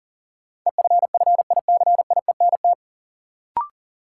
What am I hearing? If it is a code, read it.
Morse code, heard as a keyed tone, about 37 wpm, 700 Hz. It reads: EFFICIENT
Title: Sentences from Top 1300 Words - New Word Drill - 40wpm